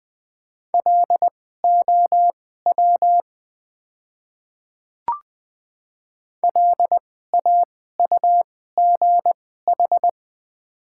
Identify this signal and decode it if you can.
Morse code, heard as a keyed tone, about 20 wpm, spaced as for 14 wpm, 700 Hz. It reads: LOW LAUGH
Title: 2x Character Spacing - Top 500 Words - 20wpm